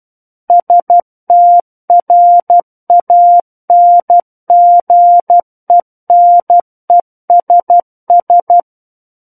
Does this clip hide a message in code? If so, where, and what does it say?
Yes: Morse code from 0.5 to 8.6 s, heard as a keyed tone, about 12 wpm, 700 Hz. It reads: STRANGENESS